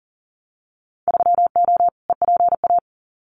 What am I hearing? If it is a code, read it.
Morse code, heard as a keyed tone, about 39 wpm, 700 Hz. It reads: 3O EPA
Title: Winter Field Day Exchanges Encore 40wpm